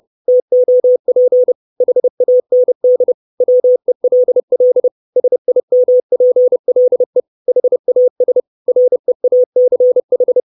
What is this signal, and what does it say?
Morse code, heard as a keyed tone, about 30 wpm, 500 Hz. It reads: TOP HAND WELL SIMPLE HAS REACH
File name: 632 Words 30WPM